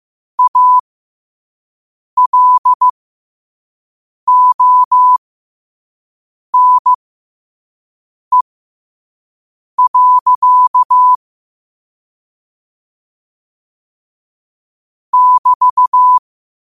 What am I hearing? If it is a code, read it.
Morse code, heard as a keyed tone, about 15 wpm, spaced as for 5 wpm, 1000 Hz. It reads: ALONE. <BT>
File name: qotdfiles / 5wpm